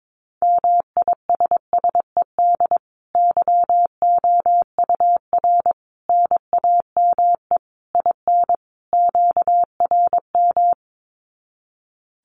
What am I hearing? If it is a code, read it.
Morse code, heard as a keyed tone, about 22 wpm, 700 Hz. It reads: MISSED YOUR NAME IN QRM
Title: Common QSO Phrases 22wpm